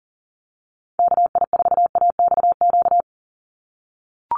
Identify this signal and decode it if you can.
Morse code, heard as a keyed tone, about 38 wpm, 700 Hz. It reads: KI4AXQ
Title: USA Call Signs 40wpm